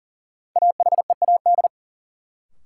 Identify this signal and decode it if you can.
Morse code, heard as a keyed tone, about 37 wpm, 700 Hz. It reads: AHEAD
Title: Sentences from Top 1100 Words - New Word Drill Alt - 40wpm